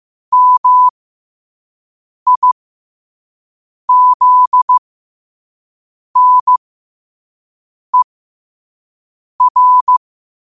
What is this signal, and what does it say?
Morse code, heard as a keyed tone, about 15 wpm, spaced as for 5 wpm, 1000 Hz. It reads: MIZNER
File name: qotdfiles / 5wpm